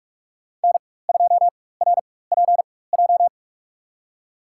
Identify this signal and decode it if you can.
Morse code, heard as a keyed tone, about 45 wpm, spaced as for 21 wpm, 700 Hz. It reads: N2RPJ
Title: USA Call Signs - 5 Characters - 4x Character Spacing - 45wpm